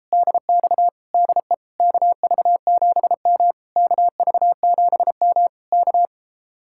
Morse code, heard as a keyed tone, about 37 wpm, 700 Hz. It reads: DX DE KV7M KV7M K